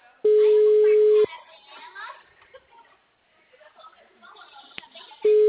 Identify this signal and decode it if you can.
Ringback tone (Europe, 425 Hz), with speech.
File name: out- -2003- - - .14830